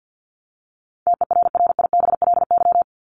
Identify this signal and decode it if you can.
Morse code, heard as a keyed tone, about 48 wpm, 700 Hz. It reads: TERRIBLY
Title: Top 1751-2000 Words 50wpm